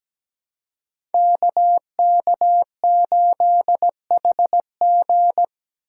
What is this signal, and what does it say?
Morse code, heard as a keyed tone, about 17 wpm, 700 Hz. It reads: KK8HG
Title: Code Groups - 5 Letter:Numbers - 17wpm